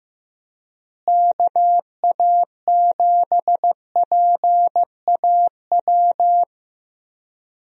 Morse code, heard as a keyed tone, about 15 wpm, 700 Hz. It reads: KA7PAW
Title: USA Call Signs - 6 Characters - Repeated 3x - 15wpm